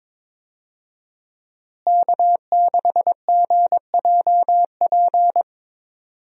Morse code, heard as a keyed tone, about 21 wpm, 700 Hz. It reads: K6GJP
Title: USA Call Signs - 5 Characters - Repeated 2x - 22wpm